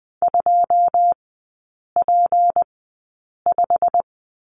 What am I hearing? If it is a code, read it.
Morse code, heard as a keyed tone, about 20 wpm, spaced as for 8 wpm, 690 Hz. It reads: 2P5
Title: 159 Lesson 26e - Letter g - Three Characters 20wpm